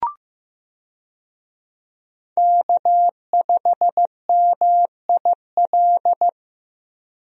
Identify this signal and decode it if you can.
Morse code, heard as a keyed tone, about 15 wpm, 700 Hz. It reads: K5MIL
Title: USA Call Signs - 5 Characters - Repeated 2x - 15wpm